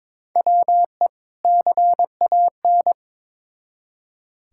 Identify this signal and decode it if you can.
Morse code, heard as a keyed tone, about 22 wpm, 700 Hz. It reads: WE CAN
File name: Build A Phrase - 2-Word Phrases - 22wpm